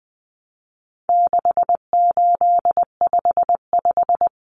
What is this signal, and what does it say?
Morse code, heard as a keyed tone, about 20 wpm, 700 Hz. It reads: 6855
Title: Common Cut Numbers - 4-Digit 20wpm